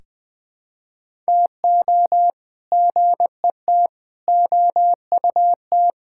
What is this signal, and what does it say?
Morse code, heard as a keyed tone, about 20 wpm, 700 Hz. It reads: TO GET OUT
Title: Sentences with 8-Word Phrases - 20wpm